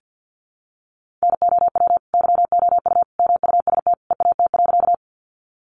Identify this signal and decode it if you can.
Morse code, heard as a keyed tone, about 50 wpm, 700 Hz. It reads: NOW YOU MUST EAT.
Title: Sentences from Top 300 Words 50wpm